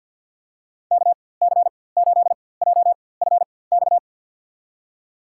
Morse code, heard as a keyed tone, about 46 wpm, spaced as for 23 wpm, 700 Hz. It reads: KC8JFX
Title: USA Call Signs - 6 Characters - 4x Character Spacing - 50wpm